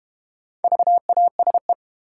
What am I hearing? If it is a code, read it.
Morse code, heard as a keyed tone, about 32 wpm, 700 Hz. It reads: VASE